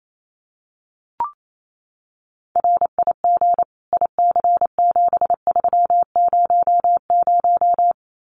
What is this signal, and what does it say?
Morse code, heard as a keyed tone, about 28 wpm, 700 Hz. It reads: RIG IC7300
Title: Common QSO Phrases Encore 28wpm